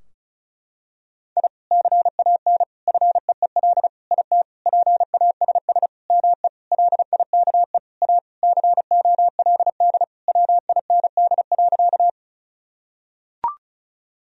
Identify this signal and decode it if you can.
Morse code, heard as a keyed tone, about 34 wpm, 700 Hz. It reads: I CAN FEEL IT PASS ME LIKE A COLD WIND.
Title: Sentences from Top 1000 Words - Review Part 2 - 35wpm